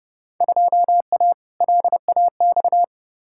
Morse code, heard as a keyed tone, about 30 wpm, 700 Hz. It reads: 2A LAX
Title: Field Day Exchanges - 30wpm